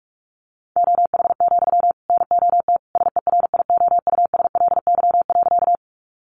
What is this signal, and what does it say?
Morse code, heard as a keyed tone, about 44 wpm, 700 Hz. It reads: OH, NOT SERIOUSLY.
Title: Sentences from Top 1200 Words 45wpm